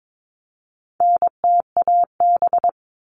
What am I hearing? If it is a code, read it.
Morse code, heard as a keyed tone, about 22 wpm, 700 Hz. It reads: NTAB
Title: All Cut Numbers - 4-Digit 22wpm